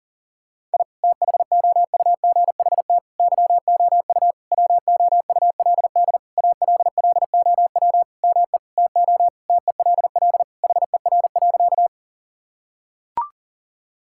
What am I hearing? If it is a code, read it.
Morse code, heard as a keyed tone, about 40 wpm, 700 Hz. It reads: I THOUGHT YOU WOULD ALLOW ME TO TELL HER.
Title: Sentences from Top 1000 Words - Review Part 5 - 40wpm